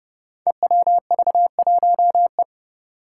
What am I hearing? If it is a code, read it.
Morse code, heard as a keyed tone, about 30 wpm, 700 Hz. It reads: EWV1E
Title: Five Challenging Letter-Number Combinations 30wpm